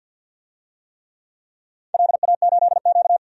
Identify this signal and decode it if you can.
Morse code, heard as a keyed tone, about 51 wpm, 700 Hz. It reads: LA8Q
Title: DXCC-100 Call Signs - 4 Characters - 50wpm